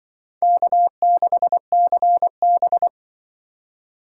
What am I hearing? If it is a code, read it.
Morse code, heard as a keyed tone, about 23 wpm, 700 Hz. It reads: K6CB
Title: Speed Racing - USA Call Signs 20wpm